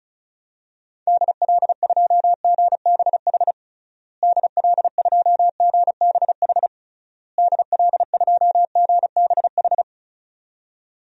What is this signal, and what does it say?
Morse code, heard as a keyed tone, about 36 wpm, 700 Hz. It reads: DL2GBH DL2GBH DL2GBH
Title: DXCC-100 Call Signs - 6 Characters - Repeated 3x - 35wpm